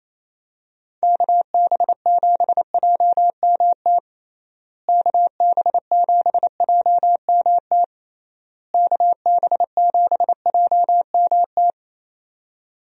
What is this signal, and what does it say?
Morse code, heard as a keyed tone, about 28 wpm, 700 Hz. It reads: KB7JMT KB7JMT KB7JMT
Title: USA Call Signs - 6 Characters - Repeated 3x - 28wpm